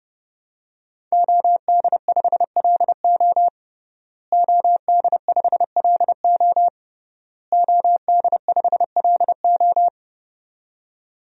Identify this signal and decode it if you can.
Morse code, heard as a keyed tone, about 30 wpm, 700 Hz. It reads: OD5LO OD5LO OD5LO
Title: DXCC-100 Call Signs - 5 Characters - Repeated 3x - 30wpm